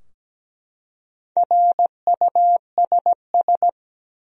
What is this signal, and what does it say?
Morse code, heard as a keyed tone, about 17 wpm, 700 Hz. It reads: RUSS
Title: USA Common Names - Max 5 Letters - 17wpm